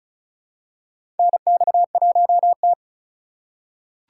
Morse code, heard as a keyed tone, about 36 wpm, 700 Hz. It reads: NX1T
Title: USA Call Signs - 4 Characters - 4x Character Spacing - 35wpm